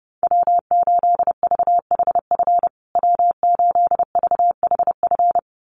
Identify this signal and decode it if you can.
Morse code, heard as a keyed tone, about 30 wpm, 700 Hz. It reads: W8VHF W8VHF
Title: USA Call Signs Repeated 2x 30wpm